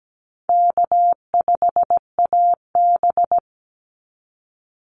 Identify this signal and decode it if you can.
Morse code, heard as a keyed tone, about 17 wpm, 700 Hz. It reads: K5AB
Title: Easy Call Signs Repeated 2x - 17wpm